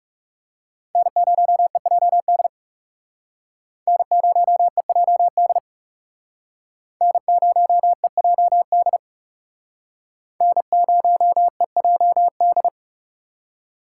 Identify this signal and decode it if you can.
Morse code, heard as a keyed tone, about 36 wpm, 700 Hz. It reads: N0EJD N0EJD N0EJD N0EJD
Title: Speed Racing - USA 1x3 Call Signs 30wpm